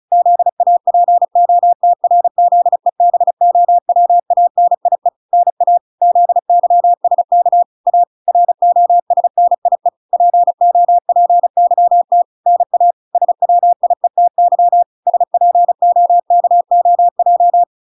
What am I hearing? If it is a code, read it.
Morse code, heard as a keyed tone, about 34 wpm, 700 Hz. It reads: ZAPOTRZEBOWANIE NA ZYSK A ROSNIE POPYT NA SWIETY SPOKOJ